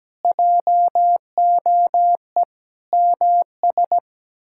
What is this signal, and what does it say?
Morse code, heard as a keyed tone, about 17 wpm, 700 Hz. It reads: JOE MS
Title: K1USN SST QSO Practice - 17wpm